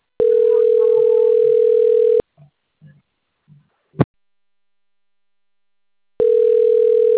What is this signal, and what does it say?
Ringback tone (North America), 440 + 480 Hz.